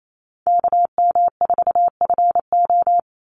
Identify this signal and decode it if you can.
Morse code, heard as a keyed tone, about 28 wpm, 700 Hz. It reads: KM4FO
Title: Call Signs Repeated 2x - 28wpm